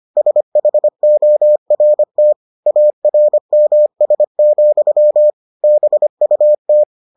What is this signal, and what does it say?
Morse code, heard as a keyed tone, about 25 wpm, 600 Hz. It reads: SHORT ARMS, BUT